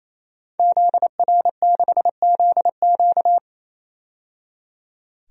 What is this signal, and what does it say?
Morse code, heard as a keyed tone, about 28 wpm, 700 Hz. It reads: ZR6ZQ
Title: DXCC-100 Call Signs - 5 Characters - 28wpm